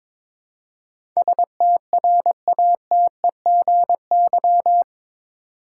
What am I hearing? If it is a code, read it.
Morse code, heard as a keyed tone, about 22 wpm, 700 Hz. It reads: STRATEGY
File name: practice / 22wpm / Eight Letter Words - 22wpm